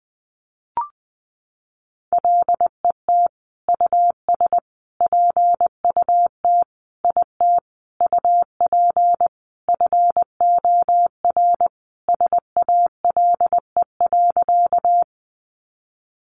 Morse code, heard as a keyed tone, about 20 wpm, 700 Hz. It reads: LET US PUT IT UP FOR SALE.
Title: Sentences from Top 1400 Words 20wpm